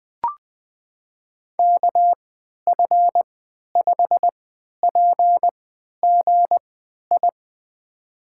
Morse code, heard as a keyed tone, about 21 wpm, spaced as for 11 wpm, 700 Hz. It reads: KF5PGI
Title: USA Call Signs - 3x Character Spacing - 20wpm